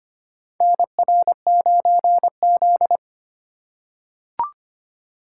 Morse code, heard as a keyed tone, about 25 wpm, 700 Hz. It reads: NR9Z
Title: USA Call Signs - 4 Characters - 2x Character Spacing - 25wpm